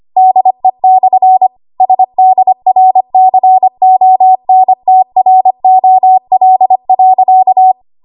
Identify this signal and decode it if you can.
Morse code, heard as a keyed tone, about 25 wpm, 750 Hz. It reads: DE/SDRCONTROL.
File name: Archive / 25 WPM / 25WPM